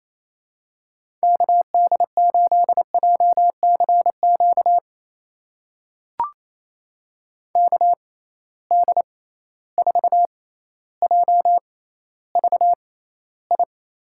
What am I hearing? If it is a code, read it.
Morse code, heard as a keyed tone, about 28 wpm, spaced as for 14 wpm, 700 Hz. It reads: KD8JCQ KD4JVI
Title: USA Call Signs - 6 Characters - 6x Character Spacing - 28wpm